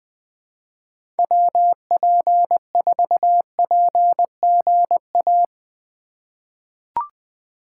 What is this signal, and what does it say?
Morse code, heard as a keyed tone, about 20 wpm, 700 Hz. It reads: WP4PGA
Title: USA Call Signs - 6 Characters - 3x Character Spacing - 20wpm